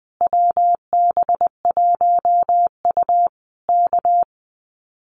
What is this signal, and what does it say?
Morse code, heard as a keyed tone, about 20 wpm, 700 Hz. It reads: WB1U K
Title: Common QSO Phrases Encore 20wpm